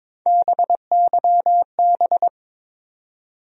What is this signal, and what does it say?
Morse code, heard as a keyed tone, about 22 wpm, 700 Hz. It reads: BYB
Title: USA Call Sign Postfixes - 3 Letters 22wpm